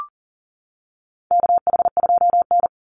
Morse code, heard as a keyed tone, about 40 wpm, 700 Hz. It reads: KH2N
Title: Call Signs - 40wpm